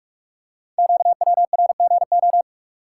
Morse code, heard as a keyed tone, about 44 wpm, 700 Hz. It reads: QWRGO
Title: Code Groups - 5 Letters - 45wpm